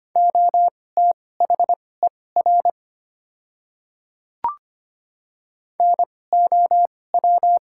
Morse code, heard as a keyed tone, about 25 wpm, spaced as for 18 wpm, 700 Hz. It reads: OTHER NOW
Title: 2x Character Spacing - Top 100 Words - 25wpm